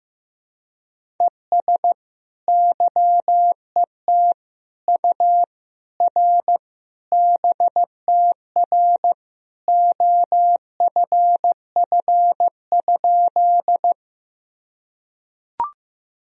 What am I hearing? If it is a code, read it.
Morse code, heard as a keyed tone, about 15 wpm, 700 Hz. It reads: ES YET U R BTR OFF?
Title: Sentences from Top 500 Words and Abbreviations 15wpm